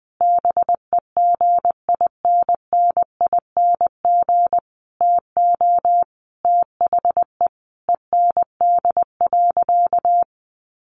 Morse code, heard as a keyed tone, about 20 wpm, 700 Hz. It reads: BEGINNING TO THE END.
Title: Sentences from Top 1300 Words 20wpm